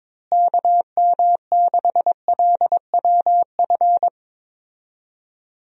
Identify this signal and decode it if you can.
Morse code, heard as a keyed tone, about 21 wpm, 700 Hz. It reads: KM6LWF
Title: USA Call Signs - 6 Characters - Repeated 2x - 22wpm